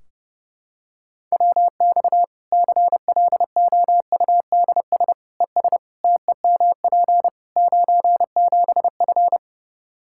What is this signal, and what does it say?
Morse code, heard as a keyed tone, about 30 wpm, 700 Hz. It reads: WX CLOUDS ES TEMP 97F
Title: Practice QSO - Exchange 2 - Info - 30wpm